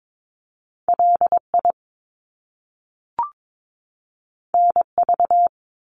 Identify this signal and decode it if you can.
Morse code, heard as a keyed tone, about 22 wpm, 700 Hz. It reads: LI NV
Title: Top Two Letter Combinations in Words 22wpm